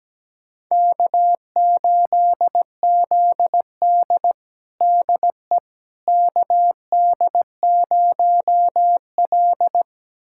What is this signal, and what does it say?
Morse code, heard as a keyed tone, about 17 wpm, 700 Hz. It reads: K8ZD DE KD0L
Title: USA Callsign DE Callsign - 4 Characters - 17wpm